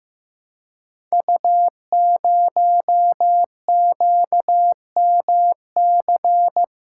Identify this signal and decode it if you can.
Morse code, heard as a keyed tone, about 15 wpm, 700 Hz. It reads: U0QMC
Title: Code Groups - 5 Letter:Numbers - 15wpm